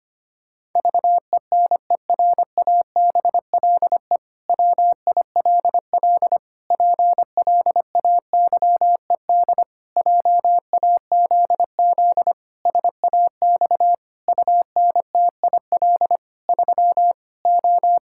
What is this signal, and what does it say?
Morse code, heard as a keyed tone, about 25 wpm, 700 Hz. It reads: VENERABLE WILL PLAYED JAZZ SAX UNTIL 3 O
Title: Pangram Challenge v2 - 25wpm